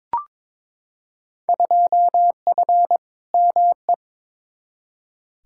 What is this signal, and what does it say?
Morse code, heard as a keyed tone, about 21 wpm, 700 Hz. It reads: 2F ME
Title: Field Day Exchanges - 22wpm